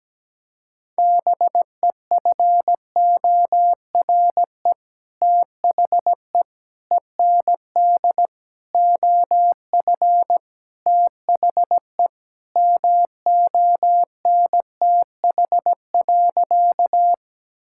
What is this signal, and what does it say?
Morse code, heard as a keyed tone, about 17 wpm, 700 Hz. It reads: BEFORE THE END OF THE MONTH.
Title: Sentences from Top 1000 Words - Review Part 5 - 17wpm